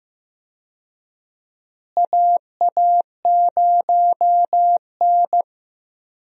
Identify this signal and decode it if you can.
Morse code, heard as a keyed tone, about 15 wpm, 700 Hz. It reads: AA0N
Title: USA Call Signs - 4 Characters - Repeated 2x - 15wpm